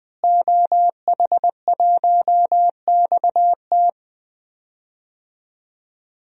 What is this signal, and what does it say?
Morse code, heard as a keyed tone, about 20 wpm, 700 Hz. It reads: OH1XT
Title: DXCC-100 Call Signs - 5 Characters - Repeated 3x - 20wpm